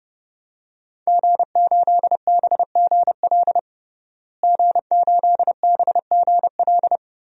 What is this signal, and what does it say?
Morse code, heard as a keyed tone, about 30 wpm, 700 Hz. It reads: G8BGL G8BGL
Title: DXCC-100 Call Signs - 5 Characters - Repeated 3x - 30wpm